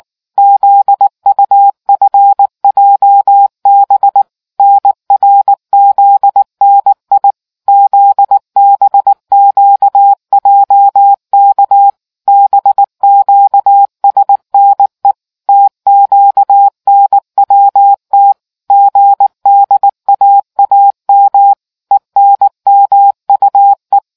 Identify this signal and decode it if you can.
Morse code, heard as a keyed tone, about 19 wpm, 800 Hz. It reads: ZUFJB NRZNI ZBQJK BQSNE TQNWT GDAAM ENMUE